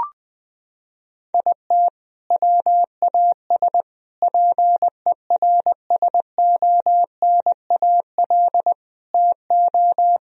Morse code, heard as a keyed tone, about 20 wpm, 700 Hz. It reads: IT WAS PERSONAL TO
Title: Sentences from Top 1000 Words - Review Part 4 - 20wpm